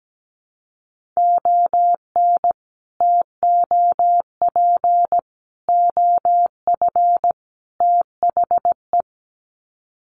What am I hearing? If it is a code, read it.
Morse code, heard as a keyed tone, about 17 wpm, 700 Hz. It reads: ON TOP OF THE
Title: Sets of 4 Words 17wpm